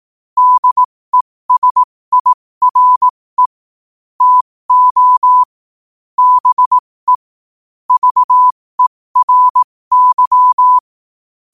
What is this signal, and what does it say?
Morse code, heard as a keyed tone, about 18 wpm, spaced as for 15 wpm, 1000 Hz. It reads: DESIRE TO BE VERY
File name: qotdfiles / 15wpm